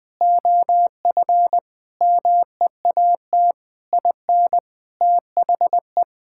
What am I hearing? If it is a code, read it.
Morse code, heard as a keyed tone, about 21 wpm, 700 Hz. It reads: OF MEAT IN THE
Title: Sentences from Top 1000 Words - Review Part 6 - 20wpm